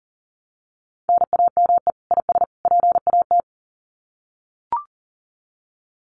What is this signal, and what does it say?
Morse code, heard as a keyed tone, about 38 wpm, 700 Hz. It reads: NAME IS PAT
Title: Common QSO Phrases 40wpm